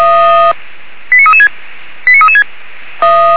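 Five-tone selective call (ZVEI), heard as quick runs of five tones, tones of about 70 ms. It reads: 89336 89336
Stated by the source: ZVEI five-tone call received by SDR.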